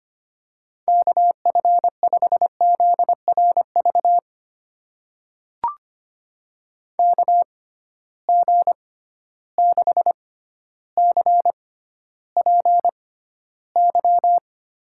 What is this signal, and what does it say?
Morse code, heard as a keyed tone, about 25 wpm, spaced as for 13 wpm, 700 Hz. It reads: KF5ZRV KG6CPY
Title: USA Call Signs - 6 Characters - 6x Character Spacing - 25wpm